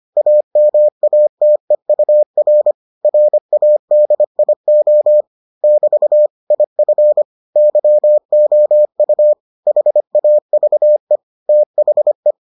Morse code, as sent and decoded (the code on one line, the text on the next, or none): .- -- .- - . ..- .-. / .-. .- -.. .. --- / -...- / .. ..-. / -.-- --- ..- / .... .- ...- . / - .... .
AMATEUR RADIO <BT> IF YOU HAVE THE